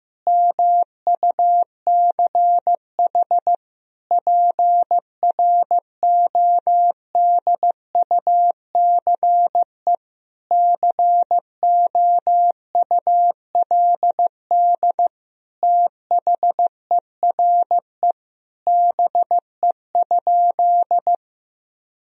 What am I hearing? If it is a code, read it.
Morse code, heard as a keyed tone, about 15 wpm, 700 Hz. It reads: MUCH PRODUCE COULD THERE BE?
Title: Sentences from Top 1000 Words - Review Part 3 - 15wpm